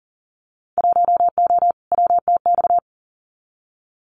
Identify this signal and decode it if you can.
Morse code, heard as a keyed tone, about 37 wpm, 700 Hz. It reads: 1O WTX
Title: Winter Field Day Exchanges Encore 40wpm